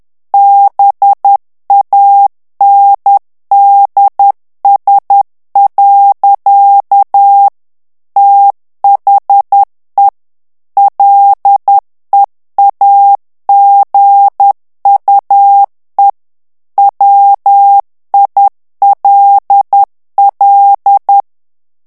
Morse code, heard as a keyed tone, about 10 wpm, 800 Hz. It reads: BANDS. THE LEAGUE WILL
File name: ANGLAIS texte clair 3 vitesse 10 mots minute